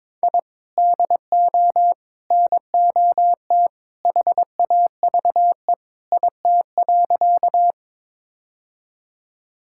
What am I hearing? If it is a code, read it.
Morse code, heard as a keyed tone, about 22 wpm, 700 Hz. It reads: I DO NOT HAVE IT.
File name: practice / 22wpm / Sentences with 3-Word Phrases - 22wpm